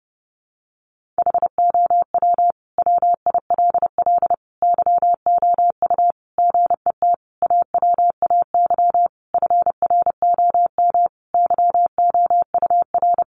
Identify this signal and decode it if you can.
Morse code, heard as a keyed tone, about 30 wpm, 700 Hz. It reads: HOW WILL YOU GET AWAY FROM YOUR
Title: Sentences from Top 1500 Words 30wpm